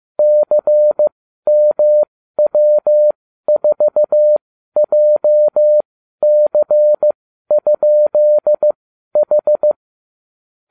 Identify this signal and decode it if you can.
Morse code, heard as a keyed tone, about 15 wpm, spaced as for 12 wpm, 600 Hz. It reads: CMW4JC?H